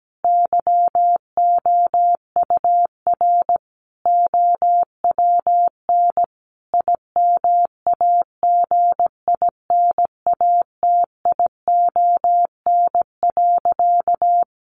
Morse code, heard as a keyed tone, about 17 wpm, 700 Hz. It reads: YOUR OWN IMAGINATION.
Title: Sentences from Top 1300 Words 17wpm